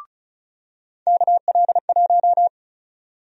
Morse code, heard as a keyed tone, about 34 wpm, 700 Hz. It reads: KL1
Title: USA Call Sign Prefixes - 35wpm